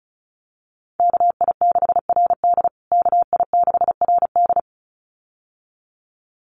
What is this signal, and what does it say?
Morse code, heard as a keyed tone, about 34 wpm, 700 Hz. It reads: KI6RD KI6RD
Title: USA Call Signs Repeated 2x 35wpm